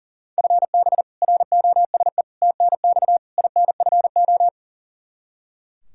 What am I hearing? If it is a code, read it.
Morse code, heard as a keyed tone, about 40 wpm, 700 Hz. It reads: FB ROSE TNX INFO
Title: Practice QSO - Exchange 3 - More Info - 40wpm